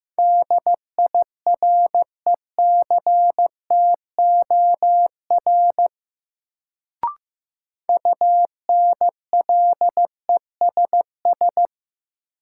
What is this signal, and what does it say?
Morse code, heard as a keyed tone, about 15 wpm, 700 Hz. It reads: DIRECTOR UNLESS
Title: Sentences from Top 1300 Words - New Word Drill Alt - 15wpm